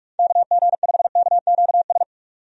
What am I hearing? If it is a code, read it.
Morse code, heard as a keyed tone, about 46 wpm, 700 Hz. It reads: KG5KQS
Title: USA Call Signs - 6 Characters - 45wpm